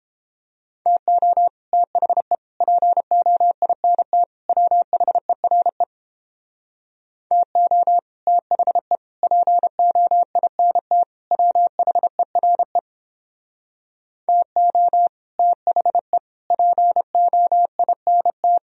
Morse code, heard as a keyed tone, about 29 wpm, 700 Hz. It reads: TO THE POINT WHERE TO THE POINT WHERE TO THE POINT